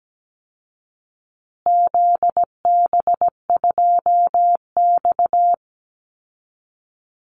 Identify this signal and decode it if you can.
Morse code, heard as a keyed tone, about 17 wpm, 700 Hz. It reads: ZB2X
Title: Call Signs Repeated 2x - 17wpm